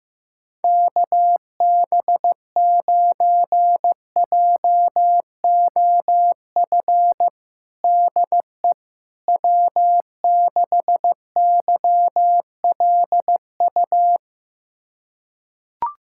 Morse code, heard as a keyed tone, about 16 wpm, 700 Hz. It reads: KB9JOF DE W6YLU
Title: USA Callsign DE Callsign 15wpm